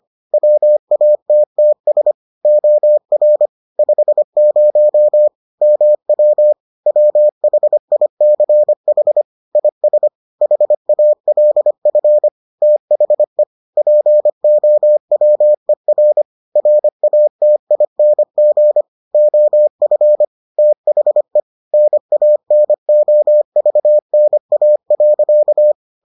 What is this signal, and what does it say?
Morse code, heard as a keyed tone, about 25 wpm, 600 Hz. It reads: WATTS OR 50 MW WHICH IS HALF THE POWER RATING OF THE NANOVNA.